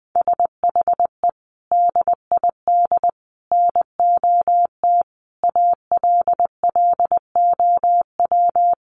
Morse code, heard as a keyed tone, about 20 wpm, 700 Hz. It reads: SHE DID NOT ALLOW
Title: Sentences from Top 1200 Words 20wpm